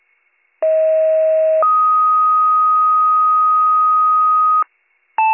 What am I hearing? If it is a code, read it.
Two-tone page, heard as a one-second tone, then a three-second tone, about 640 then 1250 Hz. It is heard from 0.6 to 4.6 s.